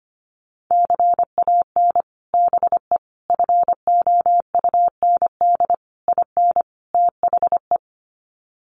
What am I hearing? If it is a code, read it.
Morse code, heard as a keyed tone, about 25 wpm, 700 Hz. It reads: CAN BE FOUND IN THE
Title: Sets of 5 Words Encore 25wpm